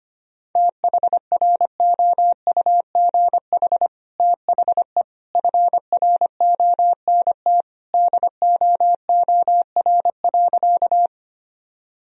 Morse code, heard as a keyed tone, about 25 wpm, 700 Hz. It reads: THROUGH THE FRONT DOOR.
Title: Sentences from Top 1000 Words - Review Part 2 - 25wpm